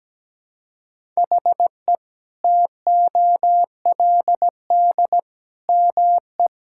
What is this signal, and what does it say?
Morse code, heard as a keyed tone, about 17 wpm, 700 Hz. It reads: HE TOLD ME
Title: Sentences from Top 1000 Words - Review Part 6 - 17wpm